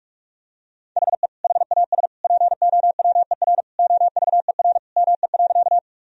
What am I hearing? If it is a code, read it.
Morse code, heard as a keyed tone, about 46 wpm, 700 Hz. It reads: HE HAS POWER OVER ME.
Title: Sentences from Top 1000 Words - Review Part 2 - 45wpm